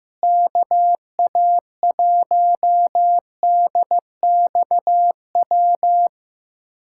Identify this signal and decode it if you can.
Morse code, heard as a keyed tone, about 15 wpm, 700 Hz. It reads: KA1DXW
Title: USA Callsign DE Callsign - 6 Characters - 15wpm